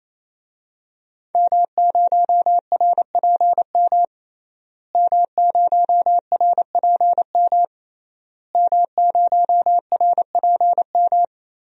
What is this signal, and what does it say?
Morse code, heard as a keyed tone, about 28 wpm, 700 Hz. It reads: M0RPM M0RPM M0RPM
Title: DXCC-100 Call Signs - 5 Characters - Repeated 3x - 28wpm